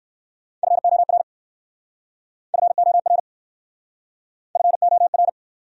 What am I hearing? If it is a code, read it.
Morse code, heard as a keyed tone, about 60 wpm, 700 Hz. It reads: FOR FOR FOR